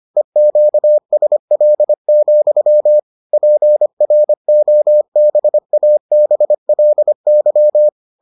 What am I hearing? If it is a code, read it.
Morse code, heard as a keyed tone, about 25 wpm, 600 Hz. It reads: EQSL, PROBABLY